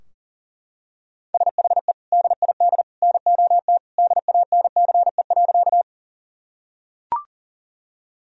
Morse code, heard as a keyed tone, about 40 wpm, 700 Hz. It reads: SHE DID NOT DANCE.
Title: Sentences from Top 1000 Words - Review Part 4 - 40wpm